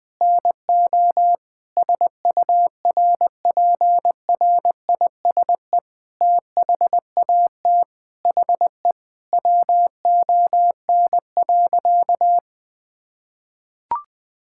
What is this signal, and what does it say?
Morse code, heard as a keyed tone, about 21 wpm, 700 Hz. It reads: NO SURPRISE THAT HE WON.
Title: Sentences with 6-Word Phrases - Spoken Beginning - 20wpm